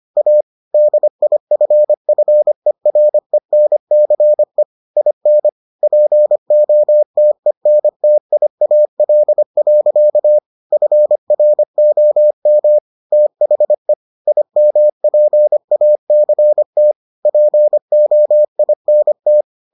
Morse code, heard as a keyed tone, about 25 wpm, 600 Hz. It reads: A DIFFERENCE IN POTENTIAL. FROM THE IMPACT POINT